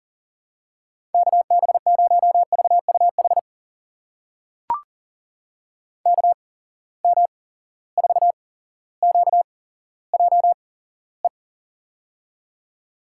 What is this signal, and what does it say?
Morse code, heard as a keyed tone, about 42 wpm, spaced as for 17 wpm, 700 Hz. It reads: KB0VUH KM4QJE
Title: USA Call Signs - 6 Characters - 8x Character Spacing - 40wpm